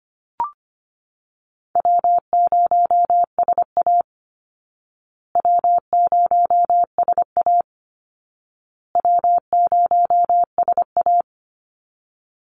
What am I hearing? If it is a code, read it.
Morse code, heard as a keyed tone, about 25 wpm, 700 Hz. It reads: W0SA W0SA W0SA
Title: Easy Call Signs Repeated 3x - 25wpm